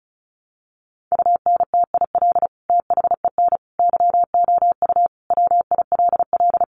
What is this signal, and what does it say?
Morse code, heard as a keyed tone, about 34 wpm, 700 Hz. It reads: UNTIL THEN YOU WILL
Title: Sentences from Top 600 Words 35wpm